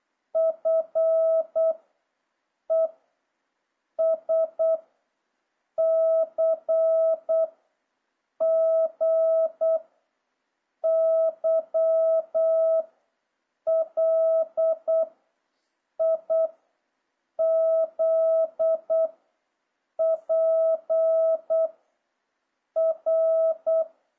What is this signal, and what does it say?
Morse code, heard as a keyed tone, about 8 wpm, 650 Hz. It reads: F E S C G Y L I Z P R